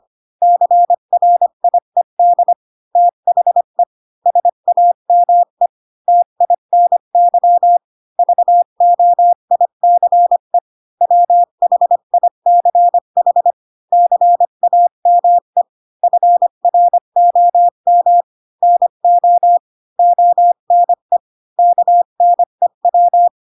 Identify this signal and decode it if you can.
Morse code, heard as a keyed tone, about 25 wpm, 700 Hz. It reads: CRIED THE SAME TINY VOICE WHICH CAME FROM NO ONE KNEW